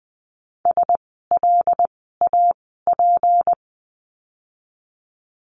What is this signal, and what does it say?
Morse code, heard as a keyed tone, about 20 wpm, spaced as for 15 wpm, 700 Hz. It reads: SLAP
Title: 153 Lesson 25f - Letter p - Words 20wpm